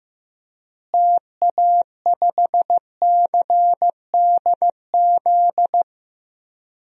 Morse code, heard as a keyed tone, about 15 wpm, 700 Hz. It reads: TA5CDZ
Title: DXCC-100 Call Signs - 6 Characters - Repeated 2x - 15wpm